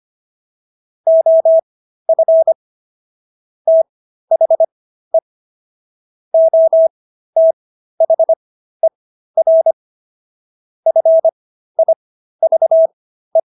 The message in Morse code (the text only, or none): OF THE OTHER FIVE